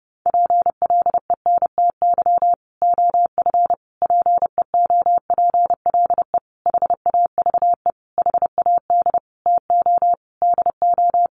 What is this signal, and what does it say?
Morse code, heard as a keyed tone, about 30 wpm, 700 Hz. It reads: PLENTY OF PEOPLE HAVE HAD TO DO
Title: Sentences from Top 1300 Words 30wpm